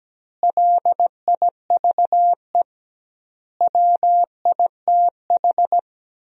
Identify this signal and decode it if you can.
Morse code, heard as a keyed tone, about 17 wpm, 700 Hz. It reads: LIVE WITH
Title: 2x Word-Spacing - Sentences from Top 500 Words - 17wpm